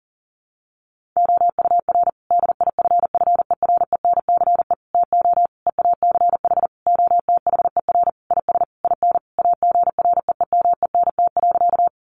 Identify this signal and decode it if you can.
Morse code, heard as a keyed tone, about 40 wpm, 700 Hz. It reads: OUR DIFFERENCE TO EACH OTHER IS IN AGREEMENT.
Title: Sentences from Top 1500 Words 40wpm